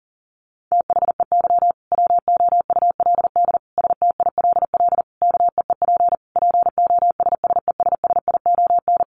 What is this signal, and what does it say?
Morse code, heard as a keyed tone, about 40 wpm, 700 Hz. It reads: THEY WOULD STILL KEEP POSSESSION
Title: Sentences from Top 1400 Words 40wpm